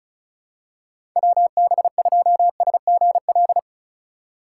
Morse code, heard as a keyed tone, about 36 wpm, 700 Hz. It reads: WB2SGL
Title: USA Call Signs - 3x Character Spacing - 35wpm